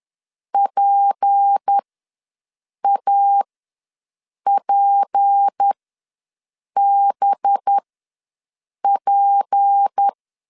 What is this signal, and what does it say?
Morse code, heard as a keyed tone, about 10 wpm, spaced as for 6 wpm, 800 Hz. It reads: PAPBP
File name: LEZIONI / 6 PKX0 / 6 P ws 10wpm 3751